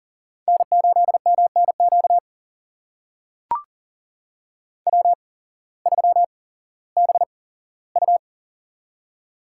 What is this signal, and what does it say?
Morse code, heard as a keyed tone, about 40 wpm, spaced as for 19 wpm, 700 Hz. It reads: N8MNQ W3BU
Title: USA Call Signs - 8x Character Spacing - 40wpm